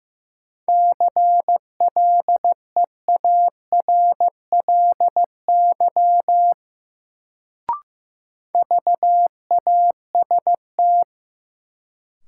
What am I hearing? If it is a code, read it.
Morse code, heard as a keyed tone, about 15 wpm, 700 Hz. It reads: CLEARLY VAST
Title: Sentences from Top 1300 Words - New Word Drill - 15wpm